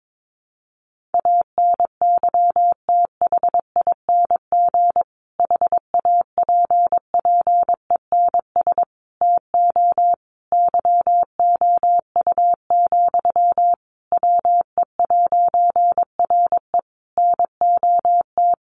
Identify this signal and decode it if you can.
Morse code, heard as a keyed tone, about 22 wpm, 700 Hz. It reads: ANYTHING HAPPENS TO YOU, WE'RE NOT
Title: A Sound Of Thunder 22wpm